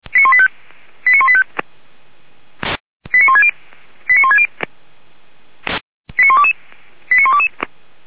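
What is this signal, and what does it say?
Five-tone selective call (ZVEI), heard as quick runs of five tones, tones of about 70 ms. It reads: 89185 89160 89160 89122 89122
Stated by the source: ZVEI five-tone call 89138 received with an SDR, fire brigade Feuerwehr Orlamünde.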